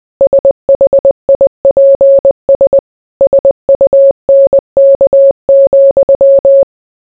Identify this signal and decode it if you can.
Morse code, heard as a keyed tone, about 20 wpm, 550 Hz. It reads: SHIPS SUNK,